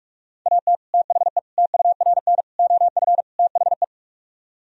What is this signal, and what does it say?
Morse code, heard as a keyed tone, about 45 wpm, 700 Hz. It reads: AT THE TURN OF THE